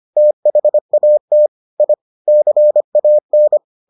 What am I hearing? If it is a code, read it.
Morse code, heard as a keyed tone, about 25 wpm, 600 Hz. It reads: THAT I CAN